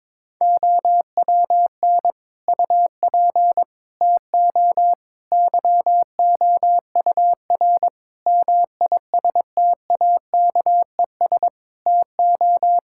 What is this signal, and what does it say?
Morse code, heard as a keyed tone, about 22 wpm, 700 Hz. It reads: OWN UP TO YOUR MISTAKES TO
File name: Difficult Sentences with Hints - 22wpm